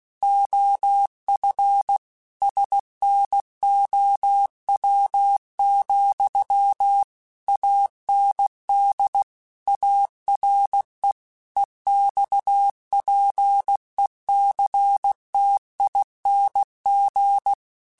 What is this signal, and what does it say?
Morse code, heard as a keyed tone, about 16 wpm, 800 Hz. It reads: OF SNOW, AND ARE EXPECTING